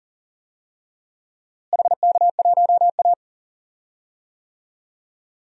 Morse code, heard as a keyed tone, about 37 wpm, 700 Hz. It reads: HK1A
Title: DXCC-100 Call Signs - 4 Characters - Repeated 2x - 40wpm